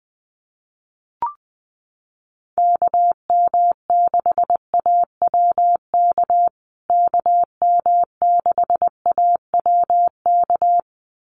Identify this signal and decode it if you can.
Morse code, heard as a keyed tone, about 20 wpm, 700 Hz. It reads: KM6AWK KM6AWK
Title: USA Call Signs Repeated 2x 20wpm